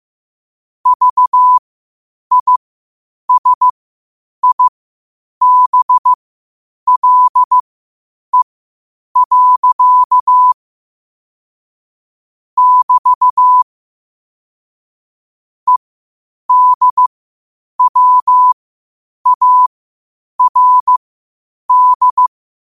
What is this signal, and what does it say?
Morse code, heard as a keyed tone, about 15 wpm, spaced as for 8 wpm, 1000 Hz. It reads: VISIBLE. <BT> EDWARD
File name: qotdfiles / 8wpm